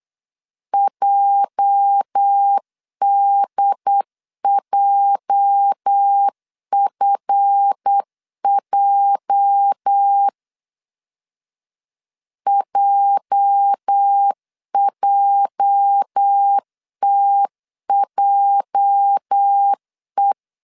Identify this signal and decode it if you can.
Morse code, heard as a keyed tone, about 8 wpm, 800 Hz. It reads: JDJFJ JJTJE
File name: LEZIONI / 8 YJ45 / 8 YJ 8wpm 3313